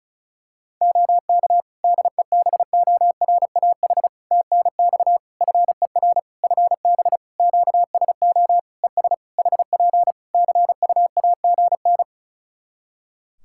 Morse code, heard as a keyed tone, about 36 wpm, 700 Hz. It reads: OK DEBORAH TNX FER FB QSO ES HP CUAGN
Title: Practice QSO - Exchange 4 - Ending - 35wpm